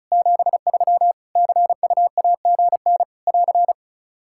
Morse code, heard as a keyed tone, about 36 wpm, 700 Hz. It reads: 73 CUAGN <AR>
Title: Practice QSO - Exchange 4 - Ending - 35wpm